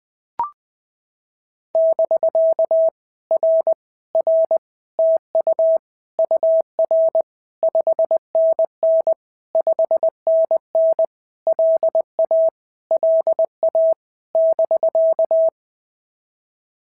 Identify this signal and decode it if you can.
Morse code, heard as a keyed tone, about 20 wpm, 650 Hz. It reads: <BK> R R TU UR 5NN 5NN LA LA <BK>
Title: POTA QSO Practice - 20wpm